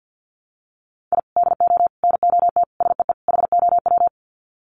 Morse code, heard as a keyed tone, about 50 wpm, 700 Hz. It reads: I DO NOT SEE HOW
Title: Sets of 5 Words 50wpm